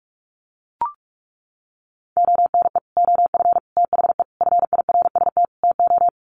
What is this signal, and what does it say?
Morse code, heard as a keyed tone, about 44 wpm, 700 Hz. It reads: ONE OF THE FIRST TO
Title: Sets of 5 Words Encore 45wpm